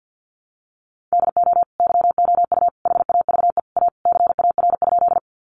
Morse code, heard as a keyed tone, about 50 wpm, 700 Hz. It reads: DO YOU HAVE A CAR?
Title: Sentences from Top 300 Words 50wpm